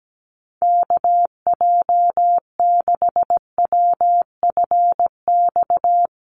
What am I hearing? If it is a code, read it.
Morse code, heard as a keyed tone, about 17 wpm, 700 Hz. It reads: KJ6WFX
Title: USA Call Signs Repeated 3x 17wpm